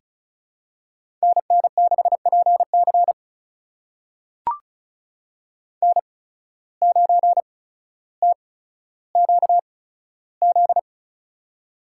Morse code, heard as a keyed tone, about 36 wpm, spaced as for 15 wpm, 700 Hz. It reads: NN6PC N9TQZ
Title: USA Call Signs - 5 Characters - 8x Character Spacing - 35wpm